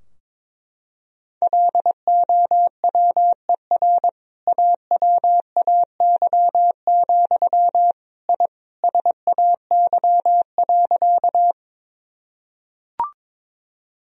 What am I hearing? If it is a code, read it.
Morse code, heard as a keyed tone, about 22 wpm, 700 Hz. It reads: LOWER AWAY, I SAY.
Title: Sentences from Top 1000 Words - Review Part 4 - 22wpm